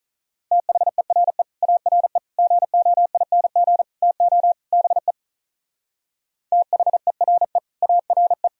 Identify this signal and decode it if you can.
Morse code, heard as a keyed tone, about 44 wpm, 700 Hz. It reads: THERE ARE GOING TO BE THERE ARE